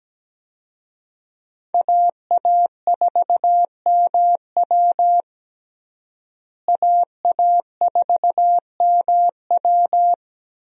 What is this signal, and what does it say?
Morse code, heard as a keyed tone, about 17 wpm, 700 Hz. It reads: AA4MW AA4MW
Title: USA Call Signs - 5 Characters - Repeated 2x - 17wpm